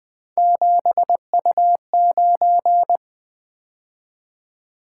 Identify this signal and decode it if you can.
Morse code, heard as a keyed tone, about 20 wpm, 700 Hz. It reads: 7U9
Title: Three Letter-Number Combinations - 20wpm